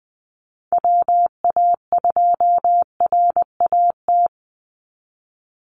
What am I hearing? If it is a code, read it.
Morse code, heard as a keyed tone, about 19 wpm, 700 Hz. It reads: WA2RAT
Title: USA Call Signs 20wpm